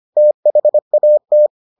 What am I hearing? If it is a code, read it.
Morse code, heard as a keyed tone, about 25 wpm, 600 Hz. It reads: THAT